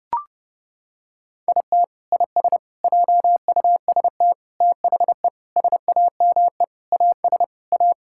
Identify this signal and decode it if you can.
Morse code, heard as a keyed tone, about 32 wpm, 700 Hz. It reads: IT IS JUST THE SAME AS A
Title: Sentences from Top 1000 Words - Review Part 6 - 30wpm